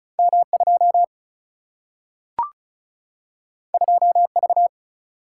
Morse code, heard as a keyed tone, about 36 wpm, 700 Hz. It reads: M 2 2 V
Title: Single-Letter Single-Number 35wpm